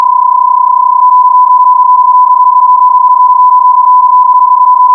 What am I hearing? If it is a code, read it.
PSK31, about 31.25 baud.
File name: hoerbeispiele / BPSK-31 Mode